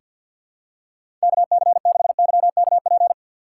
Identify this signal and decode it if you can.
Morse code, heard as a keyed tone, about 49 wpm, 700 Hz. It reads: KC6YKP
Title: USA Call Signs - 6 Characters - 4x Character Spacing - 50wpm